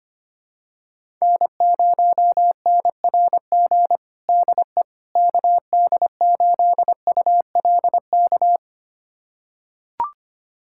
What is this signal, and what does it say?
Morse code, heard as a keyed tone, about 25 wpm, 700 Hz. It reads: N0NRG DE KD8ULK
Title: USA Callsign DE Callsign 25wpm